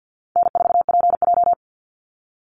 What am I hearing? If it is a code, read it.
Morse code, heard as a keyed tone, about 50 wpm, 700 Hz. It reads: N4PJ
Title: Call Signs - 50wpm